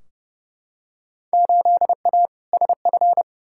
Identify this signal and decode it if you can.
Morse code, heard as a keyed tone, about 29 wpm, 700 Hz. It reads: 8A SF
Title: Field Day Exchanges Encore - 30wpm